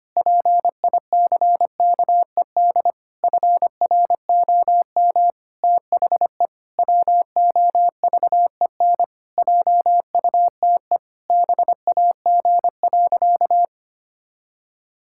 Morse code, heard as a keyed tone, about 25 wpm, 700 Hz. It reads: PICKED FROM THE WOVEN JUTE BAG.
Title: Pangram Challenge v2 - 25wpm